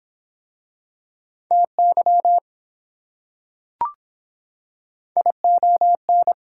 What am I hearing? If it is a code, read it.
Morse code, heard as a keyed tone, about 23 wpm, 700 Hz. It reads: TY ION